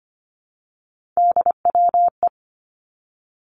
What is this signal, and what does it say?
Morse code, heard as a keyed tone, about 25 wpm, 700 Hz. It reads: DWE
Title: Three Challenging Letter-Number Combinations 25wpm